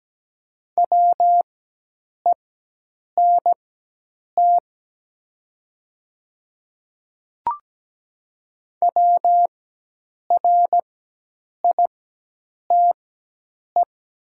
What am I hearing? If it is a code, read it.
Morse code, heard as a keyed tone, about 17 wpm, spaced as for 8 wpm, 700 Hz. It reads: WENT WRITE
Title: 4x Character Spacing - Top 300 Words - 17wpm